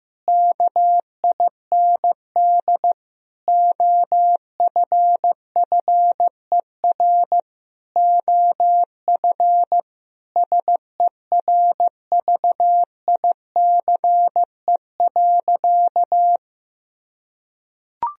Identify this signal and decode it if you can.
Morse code, heard as a keyed tone, about 15 wpm, 700 Hz. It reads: KIND OFFER OF SERVICE.
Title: Sentences from Top 1000 Words - Review Part 4 - 15wpm